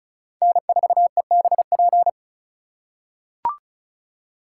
Morse code, heard as a keyed tone, about 36 wpm, 700 Hz. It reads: N4EBP
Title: USA Call Signs - 5 Characters - 3x Character Spacing - 35wpm